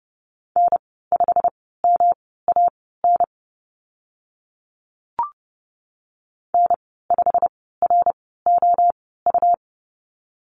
Morse code, heard as a keyed tone, about 30 wpm, spaced as for 17 wpm, 700 Hz. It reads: N5MAN N5ROU
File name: Lesson 19g - Letter w - Call Signs 30wpm